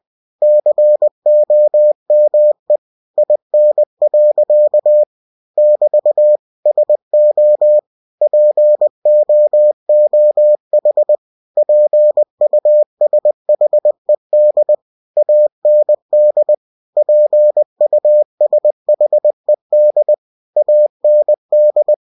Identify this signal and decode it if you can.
Morse code, heard as a keyed tone, about 20 wpm, 600 Hz. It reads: COME IN. <BT> SO POOH PUSHED AND PUSHED AND
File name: cw-books / winnie-the-pooh / winnie-20wpm-page-0015